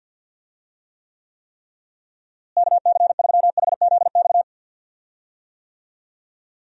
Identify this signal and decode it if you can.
Morse code, heard as a keyed tone, about 47 wpm, 700 Hz. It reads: KC3HZX
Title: USA Call Signs - 6 Characters - Repeated 3x - 50wpm